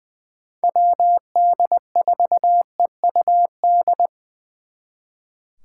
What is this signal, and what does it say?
Morse code, heard as a keyed tone, about 24 wpm, 700 Hz. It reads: WD4EUD
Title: Speed Racing - USA 2x3 Call Signs 20wpm